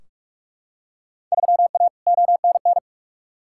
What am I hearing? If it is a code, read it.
Morse code, heard as a keyed tone, about 43 wpm, 700 Hz. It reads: 3A ONN
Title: Field Day Exchanges Encore - 45wpm